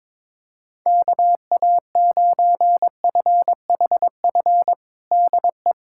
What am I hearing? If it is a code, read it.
Morse code, heard as a keyed tone, about 22 wpm, 700 Hz. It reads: KA9FHF DE
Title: USA Callsign DE Callsign - 6 Characters - 22wpm